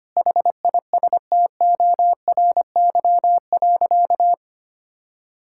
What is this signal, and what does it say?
Morse code, heard as a keyed tone, about 25 wpm, 700 Hz. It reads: HISTORY.
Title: Sentences from Top 1000 Words - Review Part 5 - 25wpm